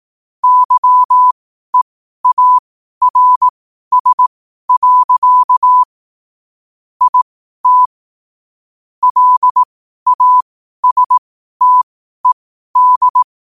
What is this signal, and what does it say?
Morse code, heard as a keyed tone, about 18 wpm, spaced as for 12 wpm, 1000 Hz. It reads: YEARS. IT LASTED
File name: qotdfiles / 12wpm